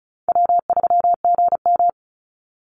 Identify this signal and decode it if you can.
Morse code, heard as a keyed tone, about 36 wpm, 700 Hz. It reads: W3GM
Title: USA Call Signs 35wpm